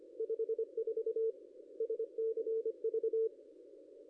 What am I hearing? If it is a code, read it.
Morse code, heard as a keyed tone, about 25 wpm, 410 Hz. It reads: 54 SCV
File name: contests / ss-cw-k5tr-op-n5rz -0600